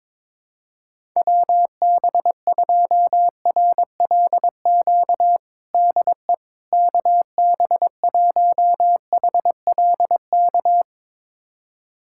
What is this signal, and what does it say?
Morse code, heard as a keyed tone, about 22 wpm, 700 Hz. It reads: WB2RLQ DE KB1HLK
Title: USA Callsign DE Callsign - 6 Characters - 22wpm